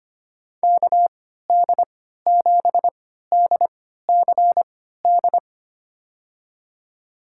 Morse code, heard as a keyed tone, about 25 wpm, spaced as for 14 wpm, 700 Hz. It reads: KD7DCD
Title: USA Call Signs - 3x Character Spacing - 25wpm